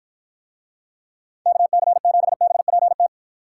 Morse code, heard as a keyed tone, about 53 wpm, 700 Hz. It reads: KC7BPT